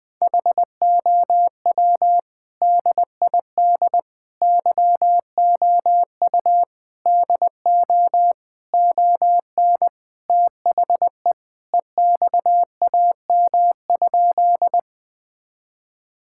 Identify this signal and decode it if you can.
Morse code, heard as a keyed tone, about 21 wpm, 700 Hz. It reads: HOW DID YOU DO ON THE EXAM?
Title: Sentences from Top 1000 Words - Review Part 5 - 20wpm